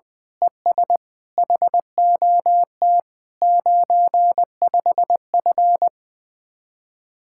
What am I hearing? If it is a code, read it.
Morse code, heard as a keyed tone, about 20 wpm, 700 Hz. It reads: ES HOT 95F
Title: Speed-Racing Common QSO Phrases - 17wpm